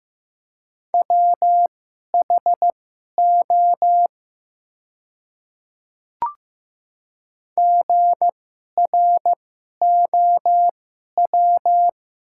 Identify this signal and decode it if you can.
Morse code, heard as a keyed tone, about 15 wpm, spaced as for 11 wpm, 700 Hz. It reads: WHO GROW
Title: 2x Character Spacing - Top 300 Words - 15wpm